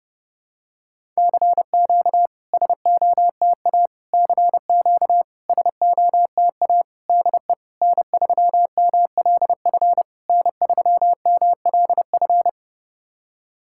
Morse code, heard as a keyed tone, about 30 wpm, 700 Hz. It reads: CQ SOTA CQ SOTA DE N3MLF N3MLF
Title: SOTA QSO Practice 30wpm